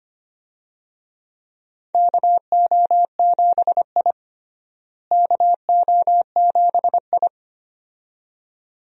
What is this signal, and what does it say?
Morse code, heard as a keyed tone, about 25 wpm, 700 Hz. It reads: KO7I KO7I
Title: USA Call Signs - 4 Characters - Repeated 2x - 25wpm